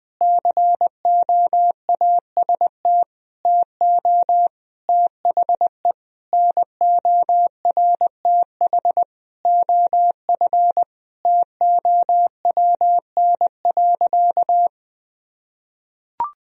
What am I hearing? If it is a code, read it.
Morse code, heard as a keyed tone, about 20 wpm, 700 Hz. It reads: COAST TO THE NORTH OF TOWN.
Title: Sentences from Top 1000 Words - Review Part 4 - 20wpm